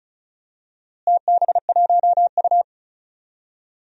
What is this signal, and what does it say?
Morse code, heard as a keyed tone, about 36 wpm, 700 Hz. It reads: TB1U
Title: DXCC-100 Call Signs - 4 Characters - Repeated 2x - 35wpm